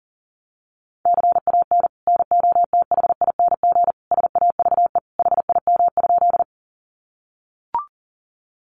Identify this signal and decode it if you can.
Morse code, heard as a keyed tone, about 37 wpm, 700 Hz. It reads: CAN NOTHING SAVE HIM?
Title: Sentences from Top 1100 Words 40wpm